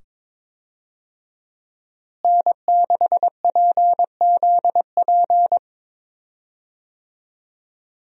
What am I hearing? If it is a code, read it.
Morse code, heard as a keyed tone, about 22 wpm, 700 Hz. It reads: N6PZP
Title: USA Call Signs - 5 Characters - Repeated 2x - 22wpm